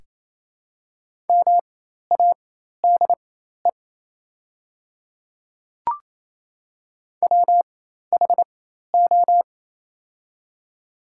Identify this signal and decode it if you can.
Morse code, heard as a keyed tone, about 28 wpm, spaced as for 13 wpm, 700 Hz. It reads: MADE WHO
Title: 4x Character Spacing - Top 200 Words - 28wpm